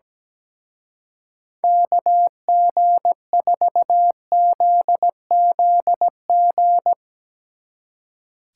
Morse code, heard as a keyed tone, about 18 wpm, 700 Hz. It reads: KG4ZZG
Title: Speed Racing - USA 2x3 Call Signs 17wpm